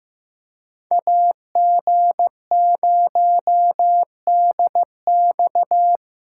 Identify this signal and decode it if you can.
Morse code, heard as a keyed tone, about 15 wpm, 700 Hz. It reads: AG0DX
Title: USA Callsign DE Callsign - 5 Characters - 15wpm